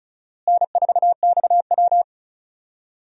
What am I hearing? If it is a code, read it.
Morse code, heard as a keyed tone, about 34 wpm, 700 Hz. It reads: N4XW
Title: USA Call Signs - 4 Characters - Repeated 2x - 35wpm